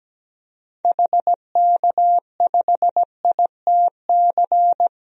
Morse code, heard as a keyed tone, about 17 wpm, 700 Hz. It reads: HK5ITC
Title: DXCC-100 Call Signs - 6 Characters - Repeated 2x - 17wpm